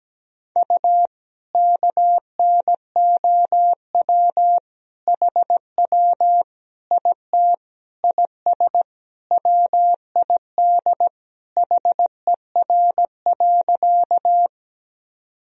Morse code, heard as a keyed tone, about 17 wpm, 700 Hz. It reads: U KNOW HW IT IS WID HER.
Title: Sentences from Top 100 Words and Abbreviations 17wpm